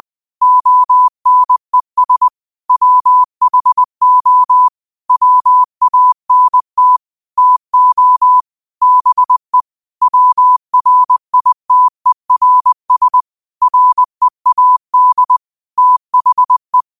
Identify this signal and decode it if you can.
Morse code, heard as a keyed tone, about 20 wpm, 1000 Hz. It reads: ONES WHO WANT TO BE WRITERS READ THE
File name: qotdfiles / 20wpm